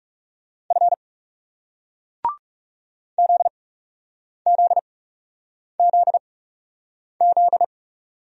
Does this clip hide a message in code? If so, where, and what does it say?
Yes: Morse code from 0.7 to 7.6 s, heard as a keyed tone, about 42 wpm, spaced as for 8 wpm, 700 Hz. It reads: F ZZZZ